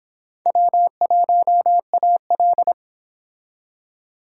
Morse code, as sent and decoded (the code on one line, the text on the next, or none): .-- .---- .- .-..
W1AL